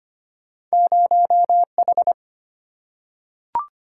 Morse code, heard as a keyed tone, about 25 wpm, 700 Hz. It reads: 0H
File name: Two Letter-Number Combinations - 25wpm